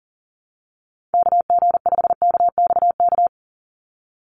Morse code, heard as a keyed tone, about 40 wpm, 700 Hz. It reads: KG5KXK
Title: USA Call Signs 40wpm